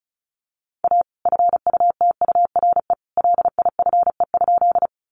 Morse code, heard as a keyed tone, about 34 wpm, 700 Hz. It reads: A FUTURE LIFE?
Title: Sentences from Top 700 Words 35wpm